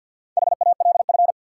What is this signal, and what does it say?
Morse code, heard as a keyed tone, about 53 wpm, 700 Hz. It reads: HALF
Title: Sets of 3 Words - New Word Drill - 50wpm